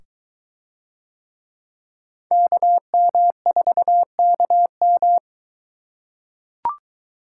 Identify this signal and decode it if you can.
Morse code, heard as a keyed tone, about 22 wpm, 700 Hz. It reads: KM4KM